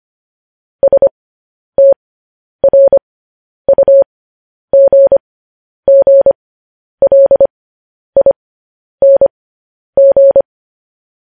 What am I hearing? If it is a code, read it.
Morse code, heard as a keyed tone, about 25 wpm, spaced as for 10 wpm, 550 Hz. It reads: STRUGGLING
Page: News Headlines in Morse Code at 10 WPM